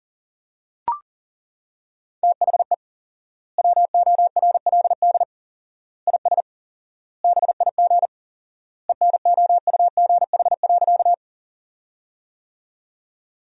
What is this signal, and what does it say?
Morse code, heard as a keyed tone, about 37 wpm, 700 Hz. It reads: THE WORLD IS BIG ENOUGH.
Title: 4x Word-Spacing - Sentences from Top 400 Words - 40wpm